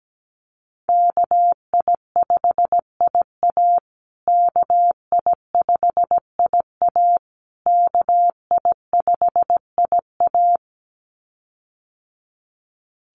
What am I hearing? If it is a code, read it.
Morse code, heard as a keyed tone, about 17 wpm, 700 Hz. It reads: KI5IA KI5IA KI5IA
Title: USA Call Signs Repeated 3x 17wpm